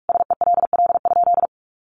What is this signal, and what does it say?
Morse code, heard as a keyed tone, about 44 wpm, 700 Hz. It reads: SELL?
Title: Sentences from Top 1100 Words 45wpm